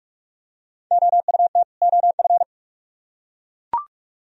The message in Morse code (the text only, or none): OUT OF